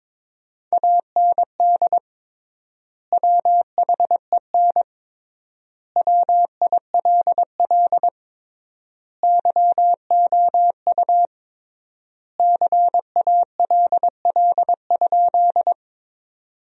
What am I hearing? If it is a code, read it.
Morse code, heard as a keyed tone, about 22 wpm, 700 Hz. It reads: AND WHEN WILL YOU CALL?
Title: 3x Word-Spacing - Sentences from Top 100 Words - 22wpm